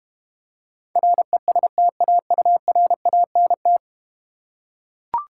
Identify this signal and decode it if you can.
Morse code, heard as a keyed tone, about 31 wpm, 700 Hz. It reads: RESTAURANT
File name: Easy Words - Speed-Racing with Hints - 25wpm